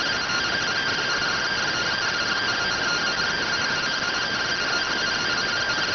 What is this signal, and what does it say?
Two-tone teleprinter signal, about 170 Hz shift.